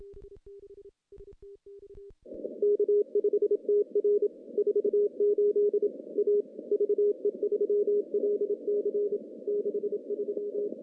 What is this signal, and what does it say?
Morse code, heard as a keyed tone, about 25 wpm, 410 Hz. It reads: K5TR 48 A VE3LC 63
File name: contests / CW-SS-k5tr- -0000